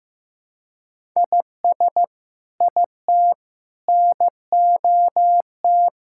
Morse code, heard as a keyed tone, about 15 wpm, 700 Hz. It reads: IS IT NOT
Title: Sentences from Top 1000 Words - Review Part 5 - 15wpm